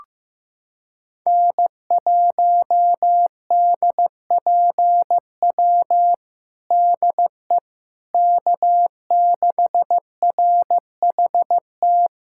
Morse code, heard as a keyed tone, about 16 wpm, 700 Hz. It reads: N1DPW DE K6RHT
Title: Practice QSO - Exchange 4 - Ending - 15wpm